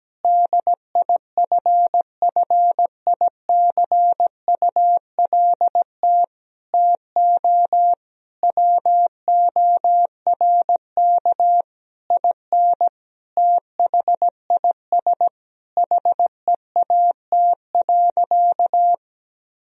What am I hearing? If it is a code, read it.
Morse code, heard as a keyed tone, about 17 wpm, 700 Hz. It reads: DIFFICULT TO WORK IN THIS HEAT.
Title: Sentences from Top 1000 Words - Review Part 3 - 17wpm